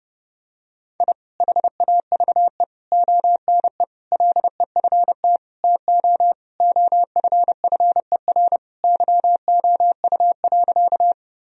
Morse code, heard as a keyed tone, about 30 wpm, 700 Hz. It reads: I HAVE ONE LEFT TO OFFER YOU.
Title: Sentences from Top 1000 Words - Review Part 4 - 30wpm